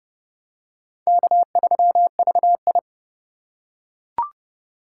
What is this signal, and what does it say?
Morse code, heard as a keyed tone, about 30 wpm, 700 Hz. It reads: K3VI
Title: USA Call Signs - 4x Character Spacing - 30wpm